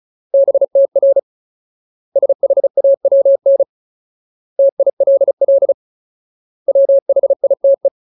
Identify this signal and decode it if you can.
Morse code, heard as a keyed tone, about 35 wpm, 550 Hz. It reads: BTR SHAWN TILL WHITE
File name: Ultimate Word List / 35WPM Ultimate